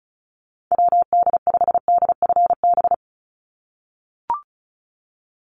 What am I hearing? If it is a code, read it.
Morse code, heard as a keyed tone, about 34 wpm, 700 Hz. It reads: WD5DFB
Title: USA Call Signs 35wpm